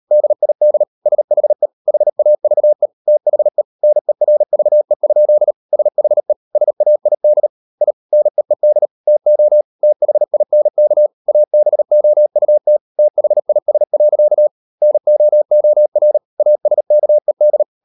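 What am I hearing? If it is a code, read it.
Morse code, heard as a keyed tone, about 38 wpm, 600 Hz. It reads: DID SHE HAVE THE NERVE? SHE SAID I NEED TO THINK ABOUT THIS. NOOR ASKED
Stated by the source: Morse code practice at 38 WPM, sending Ken Follett's novel Never in English, from CW med Gnister.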